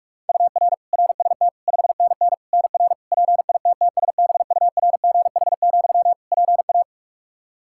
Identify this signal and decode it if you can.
Morse code, heard as a keyed tone, about 46 wpm, 700 Hz. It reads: UR RST 5NN NR PITTSBURGH, PA
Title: Practice QSO - Exchange 1 - Beginning - 45wpm